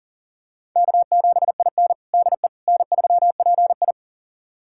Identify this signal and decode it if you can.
Morse code, heard as a keyed tone, about 37 wpm, 700 Hz. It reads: K7IN DE N3PI
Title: USA Callsign DE Callsign - 4 Characters - 40wpm